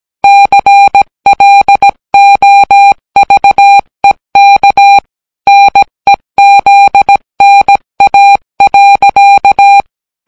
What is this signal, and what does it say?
Morse code, heard as a keyed tone, about 17 wpm, 800 Hz. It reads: CLOVEK NEZNA.